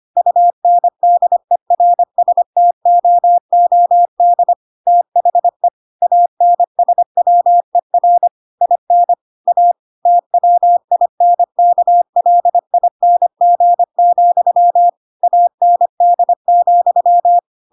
Morse code, heard as a keyed tone, about 25 wpm, 700 Hz. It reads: UNDERSTOOD THE ANSWER IN A TWINKLING, AND,
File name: Pinocchio / Pinocchio ch8 25 25 ik1hge